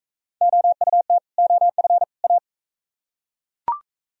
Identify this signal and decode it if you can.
Morse code, heard as a keyed tone, about 38 wpm, 700 Hz. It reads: OUT OF A